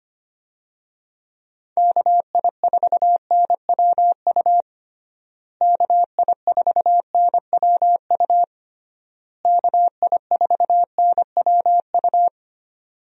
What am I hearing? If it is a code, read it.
Morse code, heard as a keyed tone, about 25 wpm, 700 Hz. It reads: KI4NWU KI4NWU KI4NWU
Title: USA Call Signs - 6 Characters - Repeated 3x - 25wpm